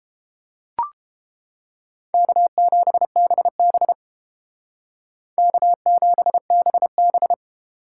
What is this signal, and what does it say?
Morse code, heard as a keyed tone, about 29 wpm, 700 Hz. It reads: K7BB K7BB